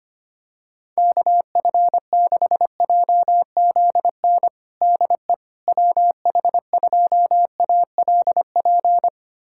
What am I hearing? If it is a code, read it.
Morse code, heard as a keyed tone, about 25 wpm, 700 Hz. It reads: KF6JZN DE WH2ALP
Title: USA Callsign DE Callsign - 6 Characters - 25wpm